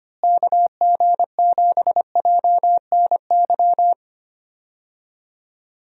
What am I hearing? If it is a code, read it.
Morse code, heard as a keyed tone, about 25 wpm, 700 Hz. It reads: KG7JNY
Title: USA Call Signs - 6 Characters - Repeated 2x - 25wpm